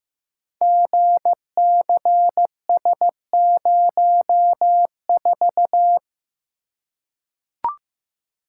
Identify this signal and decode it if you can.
Morse code, heard as a keyed tone, about 16 wpm, 700 Hz. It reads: GCS04
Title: Code Groups - 5 Letter:Numbers - 15wpm